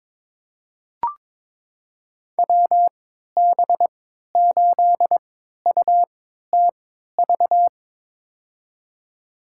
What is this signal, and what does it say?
Morse code, heard as a keyed tone, about 22 wpm, spaced as for 12 wpm, 700 Hz. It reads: WB8UTV
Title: USA Call Signs - 3x Character Spacing - 22wpm